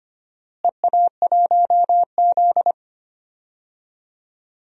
Morse code, heard as a keyed tone, about 25 wpm, 700 Hz. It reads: EA1Z
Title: DXCC-100 Call Signs - 4 Characters - Repeated 3x - 25wpm